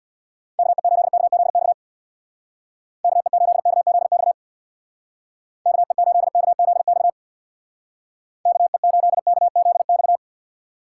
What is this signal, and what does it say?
Morse code, heard as a keyed tone, about 61 wpm, 700 Hz. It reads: KE8KZX KE8KZX KE8KZX KE8KZX
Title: Speed Racing - USA 2x3 Call Signs 50wpm